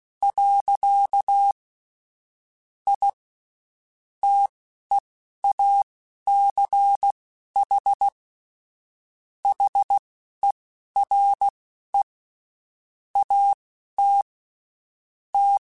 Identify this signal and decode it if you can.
Morse code, heard as a keyed tone, about 16 wpm, spaced as for 11 wpm, 800 Hz. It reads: . I TEACH HERE AT T
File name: QSOs / QSOs-mp3s / WD0BGP 1@15